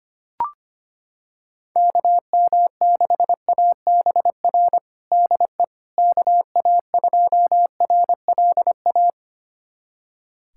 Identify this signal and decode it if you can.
Morse code, heard as a keyed tone, about 25 wpm, 700 Hz. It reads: KM6ABR DE KA2RLA
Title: USA Callsign DE Callsign 25wpm